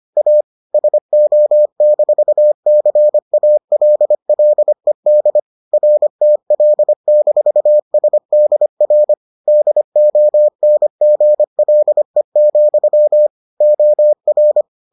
Morse code, heard as a keyed tone, about 25 wpm, 600 Hz. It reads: A SO-CALLED RTL-SDR DONGLE, OR